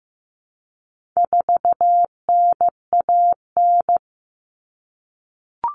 Morse code, heard as a keyed tone, about 15 wpm, 700 Hz. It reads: 4NAN
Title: Common Cut Numbers - 4-Digit 15wpm